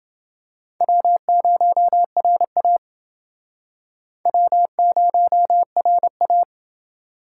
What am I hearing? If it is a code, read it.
Morse code, heard as a keyed tone, about 29 wpm, 700 Hz. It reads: W0RA W0RA